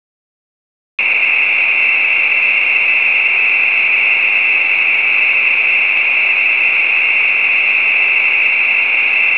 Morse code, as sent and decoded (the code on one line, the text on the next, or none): none